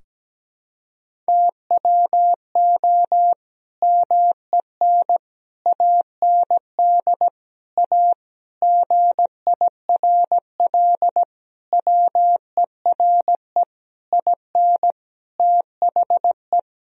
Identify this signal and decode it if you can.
Morse code, heard as a keyed tone, about 17 wpm, 700 Hz. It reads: TWO MEN AND A GIRL WERE IN THE
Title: Sentences from Top 1000 Words - Review Part 3 - 17wpm